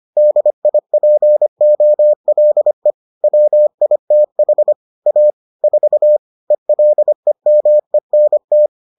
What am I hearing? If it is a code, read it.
Morse code, heard as a keyed tone, about 25 wpm, 600 Hz. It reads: DIPOLE WITH A 4 ELEMENT